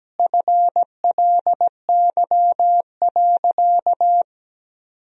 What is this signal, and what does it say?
Morse code, heard as a keyed tone, about 17 wpm, 700 Hz. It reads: FLY.
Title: Sentences from Top 1000 Words - Review Part 2 - 17wpm